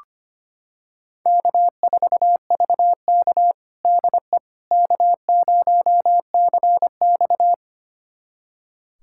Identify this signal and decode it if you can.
Morse code, heard as a keyed tone, about 25 wpm, 700 Hz. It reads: K4VK DE K0CX
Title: USA Callsign DE Callsign - 4 Characters - 25wpm